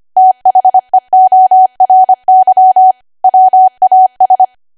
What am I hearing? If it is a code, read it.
Morse code, heard as a keyed tone, about 25 wpm, 750 Hz. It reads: THEORY WAS